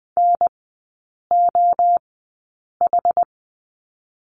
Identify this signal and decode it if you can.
Morse code, heard as a keyed tone, about 20 wpm, spaced as for 8 wpm, 700 Hz. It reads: NOH
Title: 081 Lesson 14e - Number 2 - Three Characters 20wpm